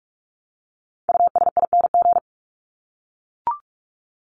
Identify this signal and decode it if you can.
Morse code, heard as a keyed tone, about 44 wpm, 700 Hz. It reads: USING
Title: ING Suffix - Top 100 Words with Base Words 45wpm